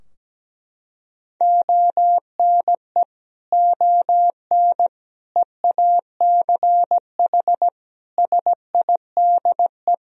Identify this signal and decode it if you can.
Morse code, heard as a keyed tone, about 17 wpm, 700 Hz. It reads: ONE ON EACH SIDE
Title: Sentences from Top 1000 Words - Review Part 2 - 17wpm